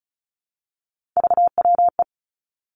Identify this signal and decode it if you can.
Morse code, heard as a keyed tone, about 36 wpm, 700 Hz. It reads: VWE
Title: Three Challenging Letter-Number Combinations 35wpm